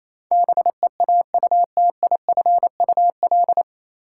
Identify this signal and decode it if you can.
Morse code, heard as a keyed tone, about 28 wpm, 700 Hz. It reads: BEAUTIFUL
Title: Sentences from Top 900 Words - New Word Drill - 28wpm